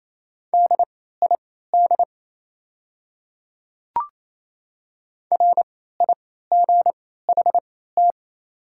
Morse code, heard as a keyed tone, about 28 wpm, spaced as for 16 wpm, 700 Hz. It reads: DID RIGHT
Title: 3x Character Spacing - Top 300 Words - 28wpm